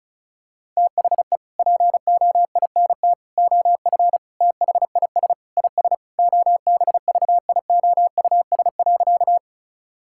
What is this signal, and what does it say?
Morse code, heard as a keyed tone, about 34 wpm, 700 Hz. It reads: THE POINT OF THIS IS OBVIOUS.
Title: Sentences from Top 1600 Words 35wpm